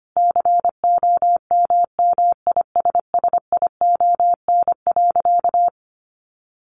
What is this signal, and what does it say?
Morse code, heard as a keyed tone, about 25 wpm, 700 Hz. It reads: COMMISSION.
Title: Sentences from Top 1400 Words 25wpm